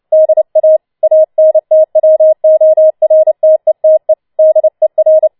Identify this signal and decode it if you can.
Morse code, heard as a keyed tone, about 29 wpm, 630 Hz. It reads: DA ANTWORTETE DER
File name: Schneewittchen Teil3 30wpm